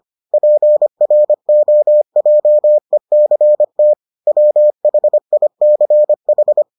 Morse code, heard as a keyed tone, about 25 wpm, 600 Hz. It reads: PROJECT WHICH